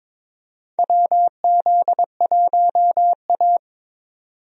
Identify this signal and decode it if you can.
Morse code, heard as a keyed tone, about 22 wpm, 700 Hz. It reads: WZ1A
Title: USA Call Signs - 4 Characters - 22wpm